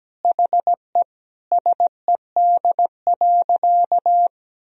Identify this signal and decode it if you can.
Morse code, heard as a keyed tone, about 17 wpm, 700 Hz. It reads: HE SED.
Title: Sentences from Top 400 Words and Abbreviations 17wpm